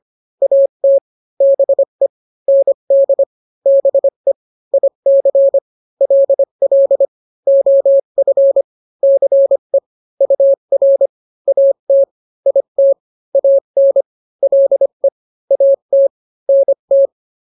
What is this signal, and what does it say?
Morse code, heard as a keyed tone, about 25 wpm, 550 Hz. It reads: AT BE ND BE IC LL OF CE UR AT IT AN LE AT NT